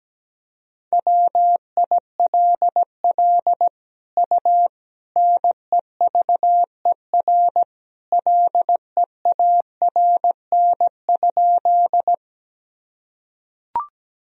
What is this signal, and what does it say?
Morse code, heard as a keyed tone, about 17 wpm, 700 Hz. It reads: WILL U NEVER LEARN?
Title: Sentences from Top 300 Words and Abbreviations 17wpm